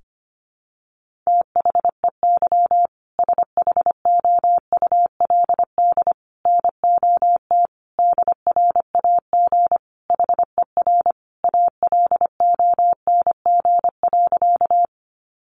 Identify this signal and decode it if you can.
Morse code, heard as a keyed tone, about 25 wpm, 700 Hz. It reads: THEY SHOULD NOT DRAG HER ALONG.
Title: Sentences from Top 1500 Words 25wpm